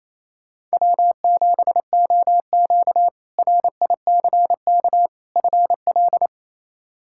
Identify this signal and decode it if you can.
Morse code, heard as a keyed tone, about 28 wpm, 700 Hz. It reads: W7OQ RICK FL
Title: K1USN SST QSO Practice - 28wpm